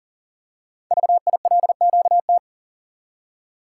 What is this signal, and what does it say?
Morse code, heard as a keyed tone, about 39 wpm, 700 Hz. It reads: VILQT
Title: Code Groups - 5 Letters - 40wpm